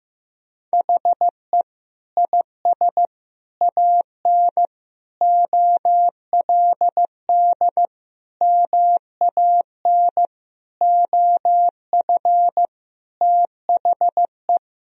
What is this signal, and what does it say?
Morse code, heard as a keyed tone, about 16 wpm, 700 Hz. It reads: HE IS AN OLD MAN OF THE
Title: Sentences from Top 1000 Words - Review Part 2 - 15wpm